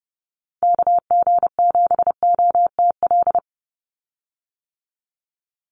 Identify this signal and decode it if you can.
Morse code, heard as a keyed tone, about 30 wpm, 700 Hz. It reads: KG7OTL
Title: USA Call Signs 30wpm